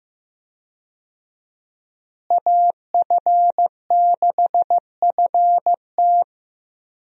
Morse code, heard as a keyed tone, about 15 wpm, 700 Hz. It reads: AF6FT
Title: USA Call Signs - 5 Characters - Repeated 3x - 15wpm